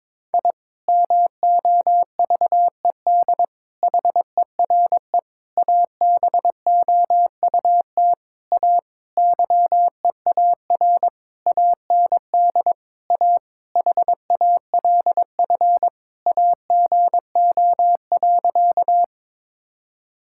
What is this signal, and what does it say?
Morse code, heard as a keyed tone, about 22 wpm, 700 Hz. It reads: I MOVED HERE ABOUT A YEAR AND A HALF AGO.
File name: Sentences with 7-Word Phrases - 22wpm